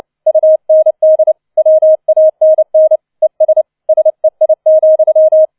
Morse code, heard as a keyed tone, about 29 wpm, 630 Hz. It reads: UND WANN ES SEI,